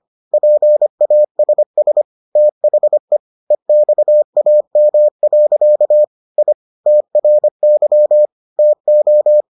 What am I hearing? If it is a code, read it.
Morse code, heard as a keyed tone, about 25 wpm, 600 Hz. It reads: PASS THE EXAM. I TRY TO